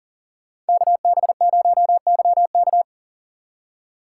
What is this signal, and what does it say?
Morse code, heard as a keyed tone, about 49 wpm, 700 Hz. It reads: KB0YK
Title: Speed Racing - USA 2x2 Call Signs 40wpm